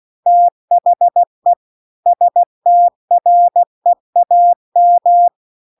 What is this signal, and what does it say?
Morse code, heard as a keyed tone, about 16 wpm, 700 Hz. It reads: THE STREAM
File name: CW-The-Big-Twoheaded-River-1-16WPM